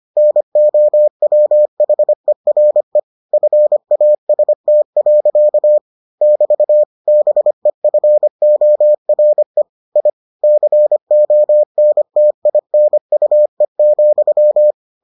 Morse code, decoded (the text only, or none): NOWHERE FAST. <BT> BEFORE I CONTINUE,